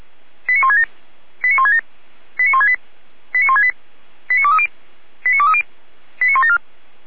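Five-tone selective call (ZVEI), heard as quick runs of five tones, tones of about 70 ms. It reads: 89167 89167 89168 89168 89230 89230 89174
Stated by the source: ZVEI five-tone call received by SDR, recorded by Feuerwehr Orlamünde fire brigade.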